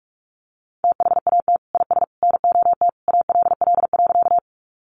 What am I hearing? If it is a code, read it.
Morse code, heard as a keyed tone, about 44 wpm, 700 Hz. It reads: THAT IS NOT ALL.
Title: Sentences from Top 600 Words 45wpm